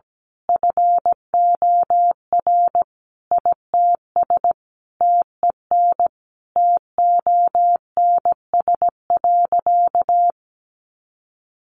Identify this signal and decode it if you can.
Morse code, heard as a keyed tone, about 17 wpm, 700 Hz. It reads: FOR ITS TEN TONS.
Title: A Sound Of Thunder 17wpm